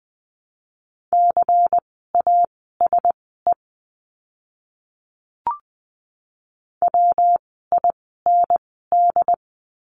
Morse code, heard as a keyed tone, about 20 wpm, spaced as for 14 wpm, 700 Hz. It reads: CASE WIND
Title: 115 Lesson 19f - Letter w - Words 20wpm